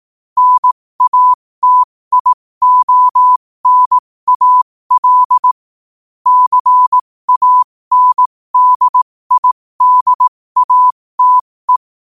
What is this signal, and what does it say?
Morse code, heard as a keyed tone, about 18 wpm, spaced as for 15 wpm, 1000 Hz. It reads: NATIONAL CANDIDATE
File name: qotdfiles / 15wpm